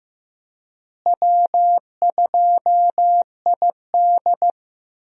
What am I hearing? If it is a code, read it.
Morse code, heard as a keyed tone, about 15 wpm, 700 Hz. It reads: W2ID
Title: USA Call Signs - 4 Characters - Repeated 3x - 15wpm